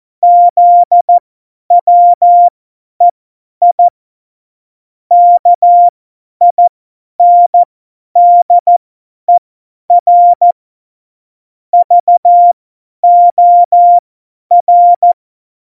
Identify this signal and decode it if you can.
Morse code, heard as a keyed tone, about 14 wpm, spaced as for 10 wpm, 700 Hz. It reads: ZWEI KINDER VOR
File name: Morsekurs / Witze / Witz 28. November 2017, 50 F70